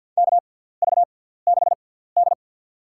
Morse code, heard as a keyed tone, about 52 wpm, spaced as for 17 wpm, 700 Hz. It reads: KV6D
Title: USA Call Signs - 4 Characters - 6x Character Spacing - 50wpm